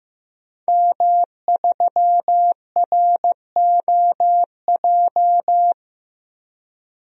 Morse code, heard as a keyed tone, about 16 wpm, 700 Hz. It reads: M3ROJ
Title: DXCC-100 Call Signs - 5 Characters - 15wpm